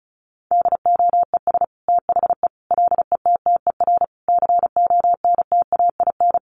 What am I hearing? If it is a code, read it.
Morse code, heard as a keyed tone, about 34 wpm, 700 Hz. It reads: DOES THE LETTER CONTAIN
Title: Sentences from Top 1100 Words 35wpm